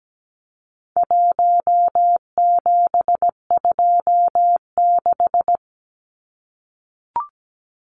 Morse code, heard as a keyed tone, about 17 wpm, 700 Hz. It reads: 1726
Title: Numbers - 4-Digit 17wpm